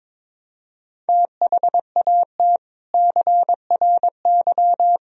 Morse code, heard as a keyed tone, about 22 wpm, 700 Hz. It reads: THAT CRY
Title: Sentences from Top 1000 Words - Review Part 3 - 22wpm